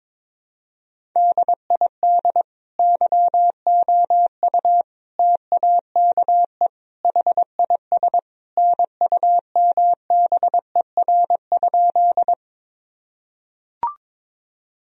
Morse code, heard as a keyed tone, about 22 wpm, 700 Hz. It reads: DID YOU TAKE HIS NUMBER?
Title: Sentences from Top 1000 Words - Review Part 1 - 22wpm